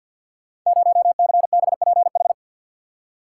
Morse code, heard as a keyed tone, about 52 wpm, 700 Hz. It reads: 0CBPH
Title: Code Groups - 5 Letter:Numbers - 50wpm